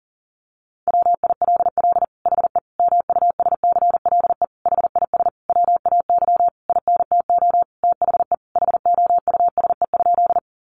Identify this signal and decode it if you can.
Morse code, heard as a keyed tone, about 40 wpm, 700 Hz. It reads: WILL HE MUSCLE HIS WAY INTO THE HOUSE?
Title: Sentences from Top 1300 Words 40wpm